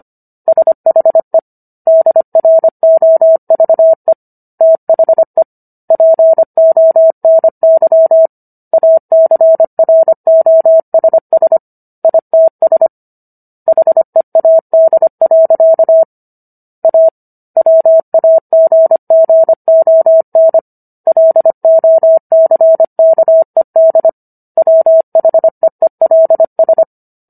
Morse code, heard as a keyed tone, about 25 wpm, 650 Hz. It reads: SHE DROVE THE PONY ACROSS ITS HEAD. A WAGGON LOCKED WHEELS